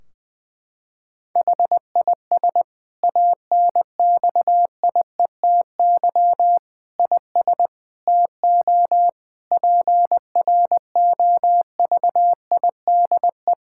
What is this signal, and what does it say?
Morse code, heard as a keyed tone, about 21 wpm, 700 Hz. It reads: HIS ANXIETY IS TO PROVIDE
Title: Sentences from Top 1600 Words 20wpm